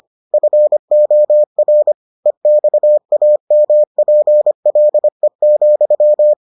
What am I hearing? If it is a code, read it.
Morse code, heard as a keyed tone, about 25 wpm, 600 Hz. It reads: FOR EXAMPLE,